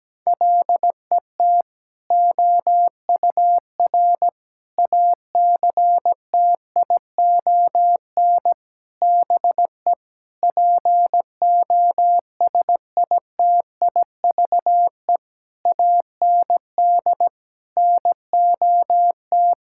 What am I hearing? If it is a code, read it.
Morse code, heard as a keyed tone, about 17 wpm, 700 Hz. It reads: LET OUR ACTION BE POSITIVE AND NOT
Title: Sentences from Top 1000 Words - Review Part 6 - 17wpm